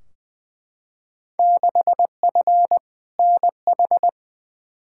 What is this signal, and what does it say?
Morse code, heard as a keyed tone, about 21 wpm, 700 Hz. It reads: 6F NH
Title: Field Day Exchanges Encore - 20wpm